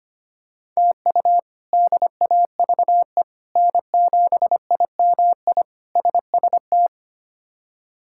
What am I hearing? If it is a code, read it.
Morse code, heard as a keyed tone, about 25 wpm, 700 Hz. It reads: TU DAVE N7IMI SST
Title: K1USN SST QSO Practice - 25wpm